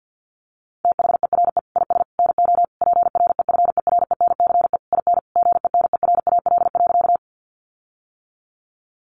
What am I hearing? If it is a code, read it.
Morse code, heard as a keyed tone, about 50 wpm, 700 Hz. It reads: THERE IS NO PREFERENCE IN GENERAL.
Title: Sentences from Top 1500 Words 50wpm